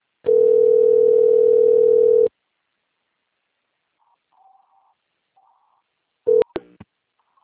Ringback tone (North America), 440 + 480 Hz.
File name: rg-1600-unknown- - - .10654